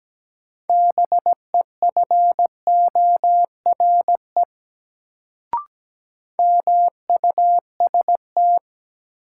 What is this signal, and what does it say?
Morse code, heard as a keyed tone, about 17 wpm, 700 Hz. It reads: BEFORE MUST
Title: Sentences from Top 200 Words - New Word Drill - 17wpm